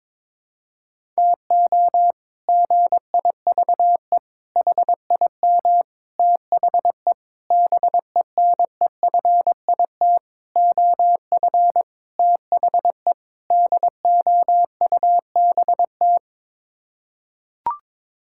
Morse code, heard as a keyed tone, about 22 wpm, 700 Hz. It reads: TO GIVE HIM THE BENEFIT OF THE DOUBT
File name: practice / 22wpm / Sets of 8 Words - 22wpm